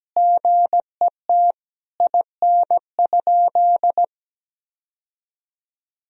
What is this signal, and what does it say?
Morse code, heard as a keyed tone, about 17 wpm, 700 Hz. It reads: GET IN?
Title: Sentences from Top 1000 Words - Review Part 2 - 17wpm